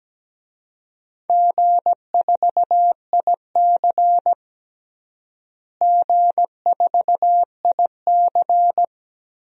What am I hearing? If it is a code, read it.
Morse code, heard as a keyed tone, about 17 wpm, 700 Hz. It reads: G4IC G4IC
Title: DXCC-100 Call Signs - 4 Characters - Repeated 3x - 17wpm